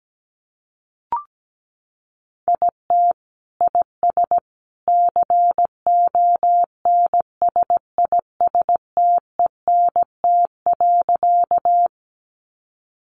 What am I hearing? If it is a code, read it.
Morse code, heard as a keyed tone, about 17 wpm, 700 Hz. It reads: IT IS CONSISTENT.
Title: Sentences from Top 1500 Words 17wpm